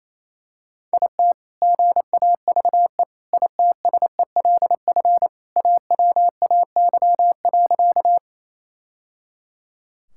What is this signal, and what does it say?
Morse code, heard as a keyed tone, about 28 wpm, 700 Hz. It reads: IT GAVE ITSELF AWAY.
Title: Sentences from Top 1000 Words - Review Part 5 - 28wpm